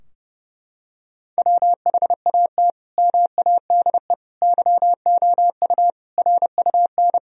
Morse code, heard as a keyed tone, about 32 wpm, 700 Hz. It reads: WHAT MADE YOU RUN
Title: Sentences from Top 1000 Words - Review Part 4 - 30wpm